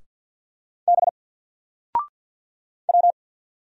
Morse code, heard as a keyed tone, about 49 wpm, 700 Hz. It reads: B W
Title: Single Letter-Number 50wpm